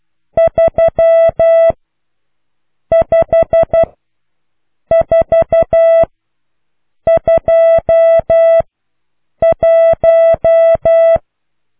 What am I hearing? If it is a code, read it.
Morse code, heard as a keyed tone, about 12 wpm, spaced as for 6 wpm, 660 Hz. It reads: 35421